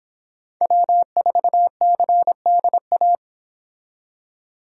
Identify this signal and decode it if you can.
Morse code, heard as a keyed tone, about 29 wpm, 700 Hz. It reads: W4CDA